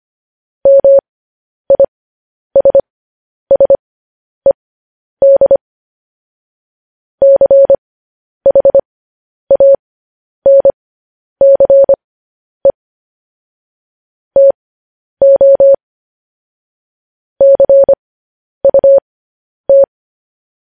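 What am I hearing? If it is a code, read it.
Morse code, heard as a keyed tone, about 25 wpm, spaced as for 10 wpm, 550 Hz. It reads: MISSED CHANCE TO CUT
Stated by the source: Morse code news headlines practice transmission.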